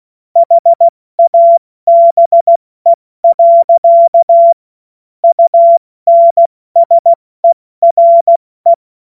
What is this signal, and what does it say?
Morse code, heard as a keyed tone, about 16 wpm, 680 Hz. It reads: HABE. UNSERE
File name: Morsekurs / Witze / Witz 06. September 2022 70 F80